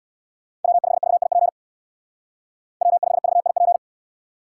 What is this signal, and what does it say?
Morse code, heard as a keyed tone, about 68 wpm, 700 Hz. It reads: W5VIP W5VIP